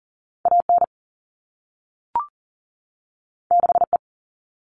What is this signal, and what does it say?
Morse code, heard as a keyed tone, about 40 wpm, 700 Hz. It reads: AN 6E
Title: All Cut Numbers - 2-Digit 40wpm